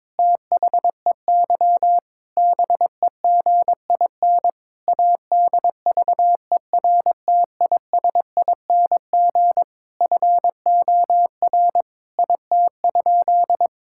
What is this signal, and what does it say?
Morse code, heard as a keyed tone, about 22 wpm, 700 Hz. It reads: THEY BEGIN ADVERTISING FOR IT?
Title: Sentences from Top 1000 Words - Review Part 6 - 22wpm